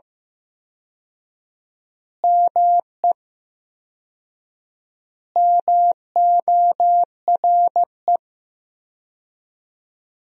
Morse code, heard as a keyed tone, about 15 wpm, 700 Hz. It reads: ME MORE
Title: 4x Word-Spacing - Sentences from Top 200 Words - 15wpm